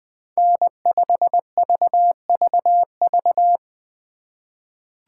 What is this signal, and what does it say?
Morse code, heard as a keyed tone, about 20 wpm, 700 Hz. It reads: N5VVV
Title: USA Call Signs - 5 Characters - 4x Character Spacing - 20wpm